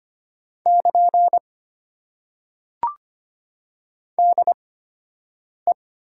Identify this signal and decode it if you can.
Morse code, heard as a keyed tone, about 25 wpm, spaced as for 7 wpm, 700 Hz. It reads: <KN> DE
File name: Top 90 QSO Elements - 8x Character Spacing - 25wpm